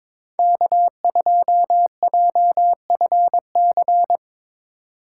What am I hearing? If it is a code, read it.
Morse code, heard as a keyed tone, about 22 wpm, 700 Hz. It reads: K2JFC
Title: USA Call Signs - 5 Characters - 8x Character Spacing - 22wpm